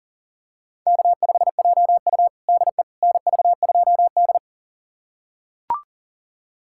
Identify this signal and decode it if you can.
Morse code, heard as a keyed tone, about 40 wpm, 700 Hz. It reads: K5JU DE NV2D
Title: USA Callsign DE Callsign - 4 Characters - 40wpm